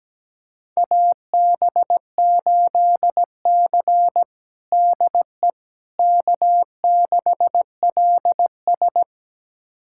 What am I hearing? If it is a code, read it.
Morse code, heard as a keyed tone, about 17 wpm, 700 Hz. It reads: AB8C DE K6LS
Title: USA Callsign DE Callsign - 4 Characters - 17wpm